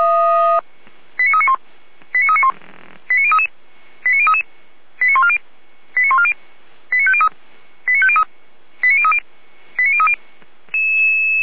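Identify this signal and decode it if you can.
Five-tone selective call (ZVEI), heard as quick runs of five tones, tones of about 70 ms. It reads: 89391 89391 89033 89033 89140 89140 89503 89503 89030 89030
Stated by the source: ZVEI five-tone call 89050 received on SDR.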